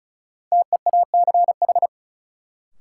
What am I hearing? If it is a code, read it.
Morse code, heard as a keyed tone, about 36 wpm, 700 Hz. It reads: TEACH
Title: Sentences from Top 500 Words - New Word Drill - 35wpm